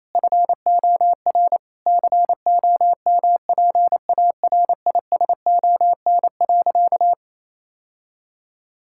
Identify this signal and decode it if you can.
Morse code, heard as a keyed tone, about 28 wpm, 700 Hz. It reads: FOR COMPARISON.
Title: Sentences from Top 1600 Words 28wpm